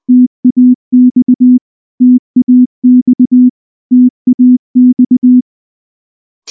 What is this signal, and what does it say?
Morse code, heard as a keyed tone, about 19 wpm, 250 Hz. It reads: TAX TAX TAX
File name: morse / morse-training-20WPM-1